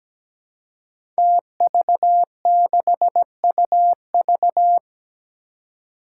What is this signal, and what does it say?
Morse code, heard as a keyed tone, about 17 wpm, 700 Hz. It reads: TV6UV
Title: Five Challenging Letter-Number Combinations 17wpm